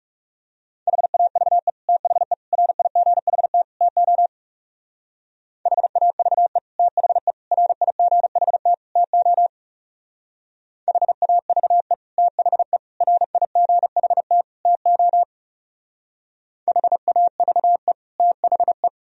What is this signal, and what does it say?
Morse code, heard as a keyed tone, about 35 wpm, 700 Hz. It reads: HAVE THE RIGHT TO HAVE THE RIGHT TO HAVE THE RIGHT TO HAVE THE
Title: Speed-Racing - Sets of 4 Words 30wpm